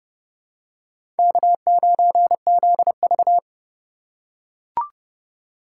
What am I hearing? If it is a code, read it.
Morse code, heard as a keyed tone, about 28 wpm, 700 Hz. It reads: K9ZV